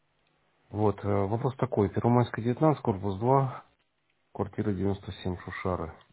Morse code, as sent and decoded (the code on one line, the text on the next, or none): none